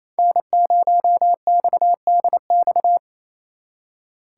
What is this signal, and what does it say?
Morse code, heard as a keyed tone, about 36 wpm, 700 Hz. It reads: N0XDX